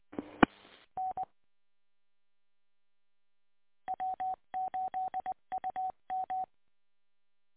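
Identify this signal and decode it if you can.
Morse code, heard as a keyed tone, about 20 wpm, 750 Hz. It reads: E N W8UM
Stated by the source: Recording archive of W8UM station, 145.230 MHz.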